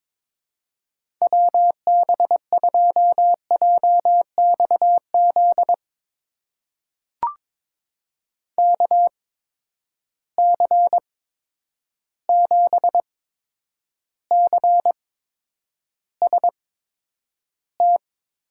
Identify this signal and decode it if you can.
Morse code, heard as a keyed tone, about 22 wpm, spaced as for 9 wpm, 700 Hz. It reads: WB2JXZ KC7CST
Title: USA Call Signs - 8x Character Spacing - 22wpm